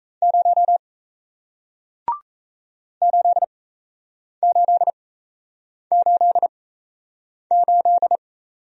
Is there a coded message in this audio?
Yes: Morse code from 0.2 to 8.2 s, heard as a keyed tone, about 38 wpm, spaced as for 8 wpm, 700 Hz. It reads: 0 8888